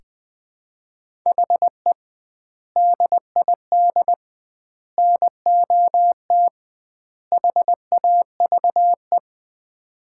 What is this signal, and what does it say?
Morse code, heard as a keyed tone, about 21 wpm, 700 Hz. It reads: HE DID NOT HAVE
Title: 2x Word-Spacing - Sets of 4 Words - 20wpm